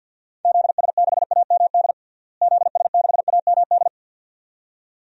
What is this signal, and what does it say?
Morse code, heard as a keyed tone, about 52 wpm, 700 Hz. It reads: ZS6AMD ZS6AMD
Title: DXCC-100 Call Signs - 6 Characters - Repeated 2x - 50wpm